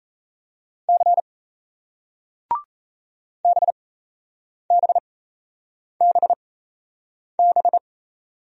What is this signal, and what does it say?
Morse code, heard as a keyed tone, about 38 wpm, spaced as for 8 wpm, 700 Hz. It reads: C BBBB